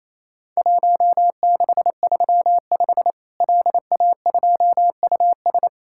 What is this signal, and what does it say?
Morse code, heard as a keyed tone, about 28 wpm, 700 Hz. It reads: 1635 LA2US
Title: Worked All Europe DX Contest - QTC Exchange - 28wpm